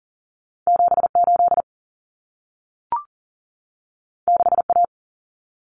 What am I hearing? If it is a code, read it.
Morse code, heard as a keyed tone, about 40 wpm, 700 Hz. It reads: 78 6A
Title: Common Cut Numbers - 2-Digit 40wpm